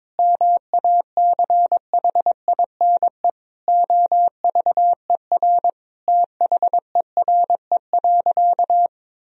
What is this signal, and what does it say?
Morse code, heard as a keyed tone, about 22 wpm, 700 Hz. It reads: MACHINE OVER THERE.
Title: Sentences from Top 1000 Words - Review Part 2 - 22wpm